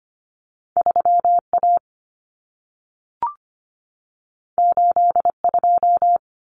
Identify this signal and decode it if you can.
Morse code, heard as a keyed tone, about 25 wpm, 700 Hz. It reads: 3A 82
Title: Common Cut Numbers - 2-Digit 25wpm